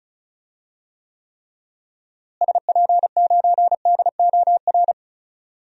Morse code, heard as a keyed tone, about 36 wpm, 700 Hz. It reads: SP9DOR
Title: DXCC-100 Call Signs - 6 Characters - Repeated 3x - 35wpm